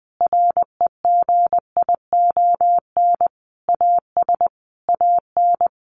Morse code, heard as a keyed tone, about 20 wpm, 700 Hz. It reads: REGION AS AN
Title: Sentences from Top 1500 Words 20wpm